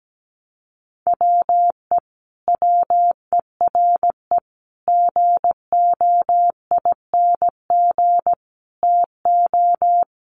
Morse code, heard as a keyed tone, about 17 wpm, 700 Hz. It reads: WE WERE GOING TO
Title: Sets of 4 Words 17wpm